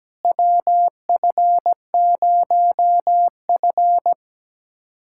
Morse code, heard as a keyed tone, about 17 wpm, 700 Hz. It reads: WF0F
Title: USA Callsign DE Callsign - 4 Characters - 17wpm